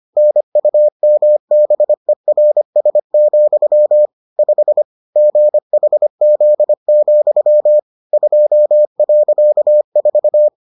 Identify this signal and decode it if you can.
Morse code, heard as a keyed tone, about 25 wpm, 600 Hz. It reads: NUMBERS, 5 GHZ, 2.4